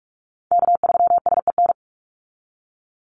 Morse code, heard as a keyed tone, about 44 wpm, 700 Hz. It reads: K3SEN
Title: Call Signs Repeated 3x - 45wpm